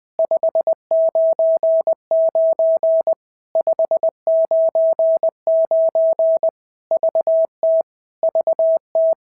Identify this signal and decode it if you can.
Morse code, heard as a keyed tone, about 20 wpm, 650 Hz. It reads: 599 599 VT VT
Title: POTA QSO Practice - 20wpm